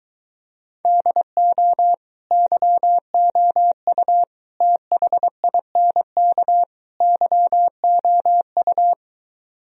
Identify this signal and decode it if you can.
Morse code, heard as a keyed tone, about 24 wpm, 700 Hz. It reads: DO YOU THINK YOU